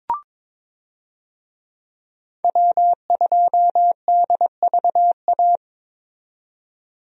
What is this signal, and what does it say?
Morse code, heard as a keyed tone, about 22 wpm, 700 Hz. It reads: W2DVA
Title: USA Call Signs - 5 Characters - 22wpm